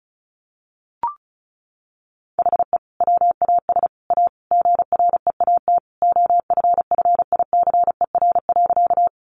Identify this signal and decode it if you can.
Morse code, heard as a keyed tone, about 34 wpm, 700 Hz. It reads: HE WAS A GREAT OFFICER.
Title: Sentences from Top 1200 Words 35wpm